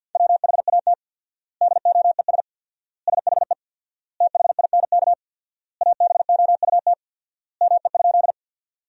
Morse code, heard as a keyed tone, about 50 wpm, 700 Hz. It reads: WHAT DOES SHE THINK ABOUT ME?
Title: 4x Word-Spacing - Sentences from Top 200 Words - 50wpm